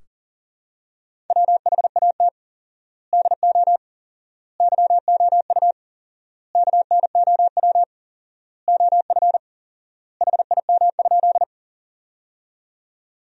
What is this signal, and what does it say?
Morse code, heard as a keyed tone, about 37 wpm, 700 Hz. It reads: WHAT DO YOU KNOW OF HIM?
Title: 4x Word-Spacing - Sentences from Top 100 Words - 40wpm